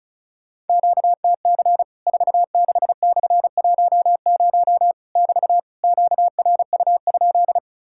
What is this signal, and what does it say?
Morse code, heard as a keyed tone, about 34 wpm, 700 Hz. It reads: QTC 46/10 <BT> QRU?
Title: Worked All Europe DX Contest - QTC Exchange - 35wpm